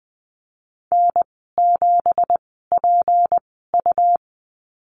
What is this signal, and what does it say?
Morse code, heard as a keyed tone, about 19 wpm, spaced as for 14 wpm, 700 Hz. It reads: N7PU
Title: 173 Lesson 28g - Number 9 - Call Signs 20wpm